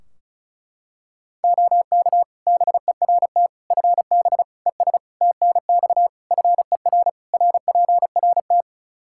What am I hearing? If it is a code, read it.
Morse code, heard as a keyed tone, about 36 wpm, 700 Hz. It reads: OK BERT FB ES TNX FER RPRT
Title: Practice QSO - Exchange 2 - Info - 35wpm